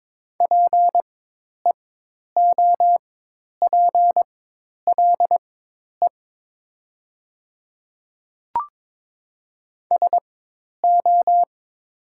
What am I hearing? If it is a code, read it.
Morse code, heard as a keyed tone, about 22 wpm, spaced as for 10 wpm, 700 Hz. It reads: PEOPLE SO
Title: 4x Character Spacing - Top 100 Words - 22wpm